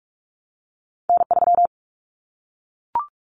Morse code, heard as a keyed tone, about 44 wpm, 700 Hz. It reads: N3
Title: Callsign Prefixes for Top 25 DXCC - Repeated 3x - 45wpm